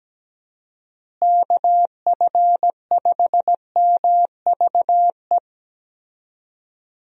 Morse code, heard as a keyed tone, about 17 wpm, 700 Hz. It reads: KF5MVE
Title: USA Call Signs - 6 Characters - Repeated 3x - 17wpm